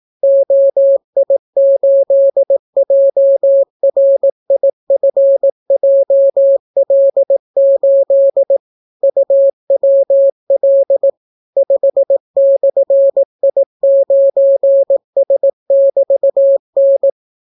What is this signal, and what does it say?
Morse code, heard as a keyed tone, about 18 wpm, 550 Hz. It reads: OI8JRIFJL8 UWL 5/I9S<BT>N